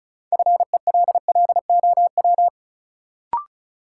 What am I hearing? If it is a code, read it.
Morse code, heard as a keyed tone, about 36 wpm, 700 Hz. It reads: FELLOW
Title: Sets of 5 Words Encore - New Word Drill - 35wpm